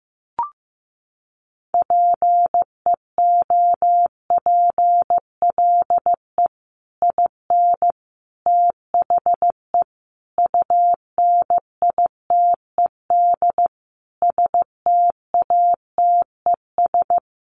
Morse code, heard as a keyed tone, about 14 wpm, 700 Hz. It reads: PEOPLE IN THE UNITED STATES
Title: Sets of 5 Words Encore 15wpm